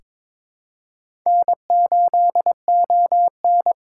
Morse code, heard as a keyed tone, about 22 wpm, 700 Hz. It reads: N8ON
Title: USA Call Signs - 4 Characters - 2x Character Spacing - 22wpm